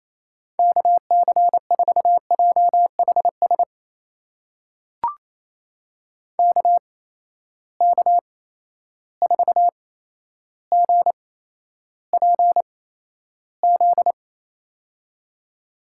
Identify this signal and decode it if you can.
Morse code, heard as a keyed tone, about 28 wpm, spaced as for 12 wpm, 700 Hz. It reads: KC4JHS KK4GPZ
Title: USA Call Signs - 6 Characters - 8x Character Spacing - 28wpm